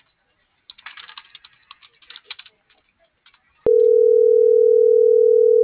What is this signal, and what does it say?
Ringback tone (North America), 440 + 480 Hz.